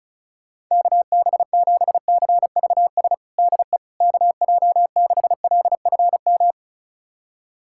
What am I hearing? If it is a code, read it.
Morse code, heard as a keyed tone, about 34 wpm, 690 Hz. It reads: KB7CVS DE KJ6LFM
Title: USA Callsign DE Callsign - 6 Characters - 35wpm